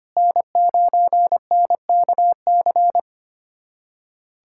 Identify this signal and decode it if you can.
Morse code, heard as a keyed tone, about 25 wpm, 700 Hz. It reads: N9NKC
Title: USA Callsign DE Callsign - 5 Characters - 25wpm